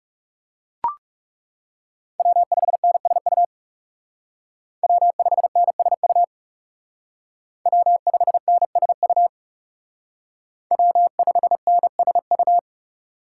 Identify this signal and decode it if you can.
Morse code, heard as a keyed tone, about 40 wpm, 700 Hz. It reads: W5NSU W5NSU W5NSU W5NSU